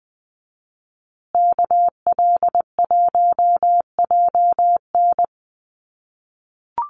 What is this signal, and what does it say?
Morse code, heard as a keyed tone, about 20 wpm, 700 Hz. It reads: KL1JN
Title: USA Call Signs Repeated 3x 20wpm